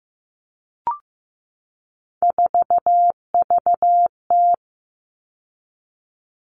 Morse code, heard as a keyed tone, about 15 wpm, 700 Hz. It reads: 4VT
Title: All Cut Numbers - 3-Digit 15wpm